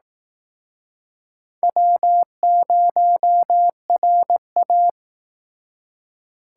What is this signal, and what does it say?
Morse code, heard as a keyed tone, about 19 wpm, 700 Hz. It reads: W0RA